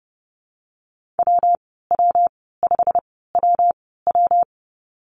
Morse code, heard as a keyed tone, about 30 wpm, spaced as for 17 wpm, 700 Hz. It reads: WW5WW
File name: Lesson 19g - Letter w - Call Signs 30wpm